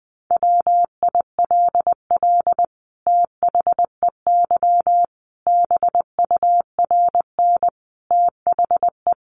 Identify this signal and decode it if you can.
Morse code, heard as a keyed tone, about 19 wpm, 700 Hz. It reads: WILL THEY BURN THE
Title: Sentences from Top 1300 Words 20wpm